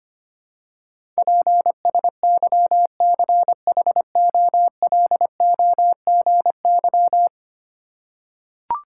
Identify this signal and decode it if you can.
Morse code, heard as a keyed tone, about 25 wpm, 700 Hz. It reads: PSYCHOLOGY
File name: 10 Letter Words - First 3 Letters Given - 25wpm